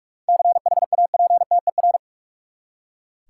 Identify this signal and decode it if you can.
Morse code, heard as a keyed tone, about 45 wpm, 700 Hz. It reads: CHAPTER
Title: Seven Letter Words - 45wpm